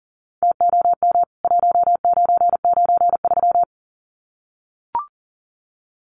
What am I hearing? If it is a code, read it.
Morse code, heard as a keyed tone, about 40 wpm, 700 Hz. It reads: TOM 1993
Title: CWT Exchange 40wpm